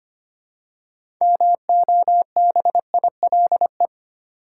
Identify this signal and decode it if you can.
Morse code, heard as a keyed tone, about 25 wpm, 700 Hz. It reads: MOBILE
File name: practice / 25wpm / Easy Words with Hints - 25wpm